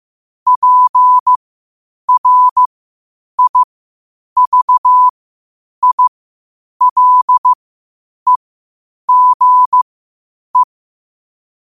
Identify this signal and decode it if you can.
Morse code, heard as a keyed tone, about 15 wpm, spaced as for 8 wpm, 1000 Hz. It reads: PRIVILEGE
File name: qotdfiles / 8wpm